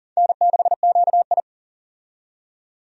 Morse code, heard as a keyed tone, about 40 wpm, 700 Hz. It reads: N6QI
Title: USA Call Signs - 4 Characters - 40wpm